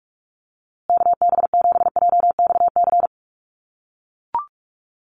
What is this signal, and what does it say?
Morse code, heard as a keyed tone, about 44 wpm, 690 Hz. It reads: KB7JXC
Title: USA Call Signs Repeated 3x 45wpm